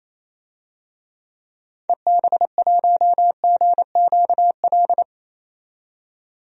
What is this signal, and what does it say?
Morse code, heard as a keyed tone, about 28 wpm, 700 Hz. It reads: EB1GQL
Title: DXCC-100 Call Signs - 6 Characters - 28wpm